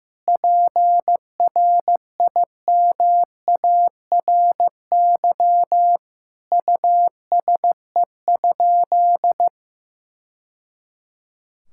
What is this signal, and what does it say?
Morse code, heard as a keyed tone, about 15 wpm, 700 Hz. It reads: PRIMARY USE?
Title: Sentences from Top 1000 Words - Review Part 5 - 15wpm